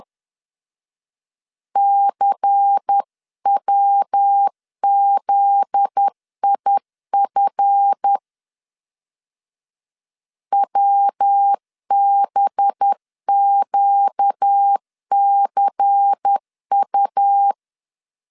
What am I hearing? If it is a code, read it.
Morse code, heard as a keyed tone, about 10 wpm, 800 Hz. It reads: CWZIF WBQCU
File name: LEZIONI / 7 GF9Z / 7 GF9Z 10wpm 2611